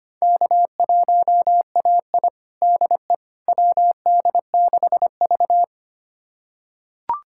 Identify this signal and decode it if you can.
Morse code, heard as a keyed tone, about 25 wpm, 700 Hz. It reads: K1AI DE WD6V
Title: USA Callsign DE Callsign - 4 Characters - 25wpm